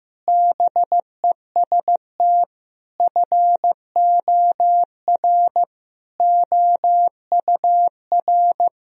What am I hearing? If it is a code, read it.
Morse code, heard as a keyed tone, about 15 wpm, 700 Hz. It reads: BEST FOR OUR
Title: Sentences from Top 1000 Words - Review Part 4 - 15wpm